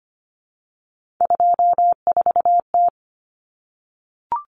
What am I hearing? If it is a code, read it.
Morse code, heard as a keyed tone, about 25 wpm, 700 Hz. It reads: 24T
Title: Common Cut Numbers - 3-Digit 25wpm